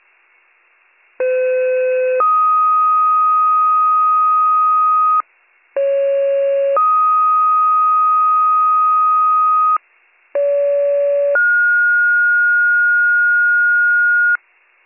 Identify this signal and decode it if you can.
Two-tone page, heard as a one-second tone, then a three-second tone, about 510 then 1250 Hz, about 550 then 1250 Hz, about 570 then 1430 Hz. It is heard from 1.2 to 14.4 s.